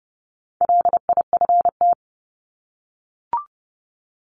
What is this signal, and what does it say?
Morse code, heard as a keyed tone, about 28 wpm, 700 Hz. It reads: LIFT
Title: Top 751-1000 Words 30wpm